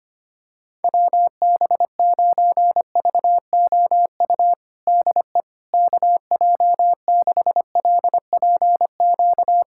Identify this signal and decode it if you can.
Morse code, heard as a keyed tone, about 25 wpm, 700 Hz. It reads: WB9VOU DE KJ6LPQ
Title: USA Callsign DE Callsign - 6 Characters - 25wpm